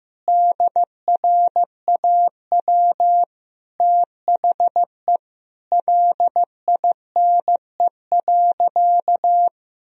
Morse code, heard as a keyed tone, about 15 wpm, 700 Hz. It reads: DRAW THE LINE.
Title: Sentences from Top 1000 Words - Review Part 1 - 15wpm